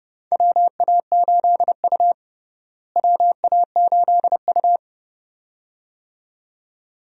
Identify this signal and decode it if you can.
Morse code, heard as a keyed tone, about 30 wpm, 700 Hz. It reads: WA8U WA8U
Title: USA Call Signs - 4 Characters - Repeated 2x - 30wpm